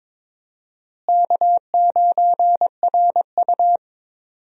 Morse code, heard as a keyed tone, about 22 wpm, 700 Hz. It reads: K9RU
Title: USA Call Signs - 4 Characters - Repeated 2x - 22wpm